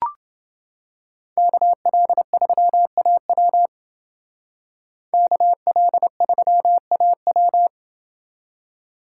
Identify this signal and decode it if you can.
Morse code, heard as a keyed tone, about 29 wpm, 700 Hz. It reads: KL3AW KL3AW